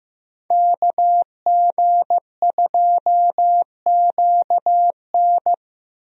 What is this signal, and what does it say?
Morse code, heard as a keyed tone, about 15 wpm, 700 Hz. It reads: KG2QN
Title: USA Call Signs - 5 Characters - Repeated 2x - 15wpm